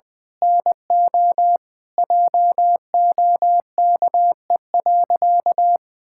Morse code, heard as a keyed tone, about 20 wpm, 700 Hz. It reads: NO JOKE.
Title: Pangram Challenge v2 - 20wpm